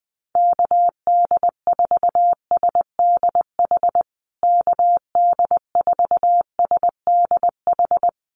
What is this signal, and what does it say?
Morse code, heard as a keyed tone, about 20 wpm, 700 Hz. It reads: KD4SDH KD4SDH
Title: USA Call Signs Repeated 2x 20wpm